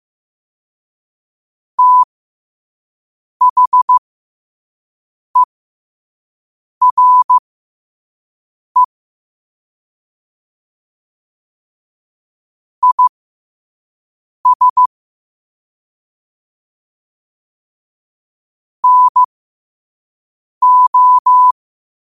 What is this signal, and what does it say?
Morse code, heard as a keyed tone, about 15 wpm, spaced as for 5 wpm, 1000 Hz. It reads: THERE IS NO
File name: 5wpm